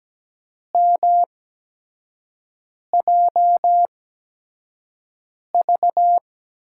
Morse code, heard as a keyed tone, about 17 wpm, spaced as for 5 wpm, 700 Hz. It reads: MJV
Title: USA Call Signs - 5 Characters - 8x Character Spacing - 17wpm